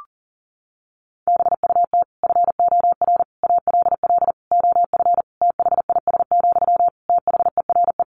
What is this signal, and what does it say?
Morse code, heard as a keyed tone, about 40 wpm, 700 Hz. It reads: BUT FOR ALL OF THIS, THERE
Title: Sentences from Top 1300 Words 40wpm